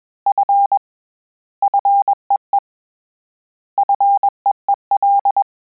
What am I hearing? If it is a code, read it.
Morse code, heard as a keyed tone, about 21 wpm, 800 Hz. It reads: F FEE FEEL
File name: Combinations / Combinations-mp3s / Combinations@20